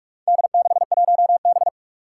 Morse code, heard as a keyed tone, about 46 wpm, 700 Hz. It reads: D61B
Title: Four Challenging Letter-Number Combinations 45wpm